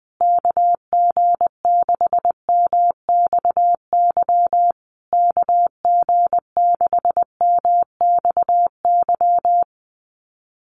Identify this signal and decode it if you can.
Morse code, heard as a keyed tone, about 20 wpm, 700 Hz. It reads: KG6MXY KG6MXY
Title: USA Call Signs Repeated 3x 20wpm